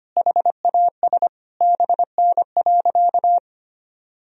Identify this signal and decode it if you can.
Morse code, heard as a keyed tone, about 25 wpm, 700 Hz. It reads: HAS BN.
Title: Sentences from Top 300 Words and Abbreviations 25wpm